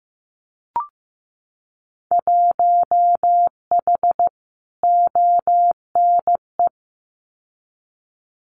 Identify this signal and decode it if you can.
Morse code, heard as a keyed tone, about 15 wpm, 700 Hz. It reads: 1H ONE
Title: Winter Field Day Exchanges Encore 15wpm